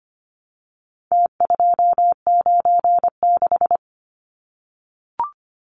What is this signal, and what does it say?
Morse code, heard as a keyed tone, about 25 wpm, 700 Hz. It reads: T296
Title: Numbers - Contest Style - 4-Digit 25wpm